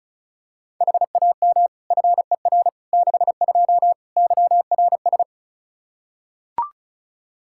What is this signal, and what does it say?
Morse code, heard as a keyed tone, about 36 wpm, 700 Hz. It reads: HAM FER 62 YRS
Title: Practice QSO - Exchange 3 - More Info - 35wpm